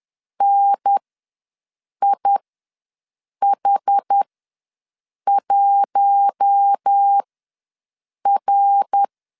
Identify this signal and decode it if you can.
Morse code, heard as a keyed tone, about 10 wpm, spaced as for 6 wpm, 800 Hz. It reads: NIH1R
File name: LEZIONI / 5 LBVC / 5 L ws 10wpm 3636